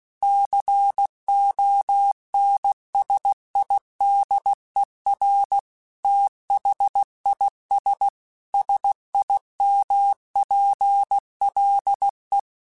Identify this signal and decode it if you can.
Morse code, heard as a keyed tone, about 16 wpm, 800 Hz. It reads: CONSIDER THIS SIMPLE
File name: Technical Subjects / Technical Subjects-mp3s / OR Gate @15